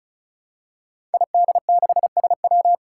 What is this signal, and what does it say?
Morse code, heard as a keyed tone, about 36 wpm, 700 Hz. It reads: ID6SW
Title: Five Challenging Letter-Number Combinations 35wpm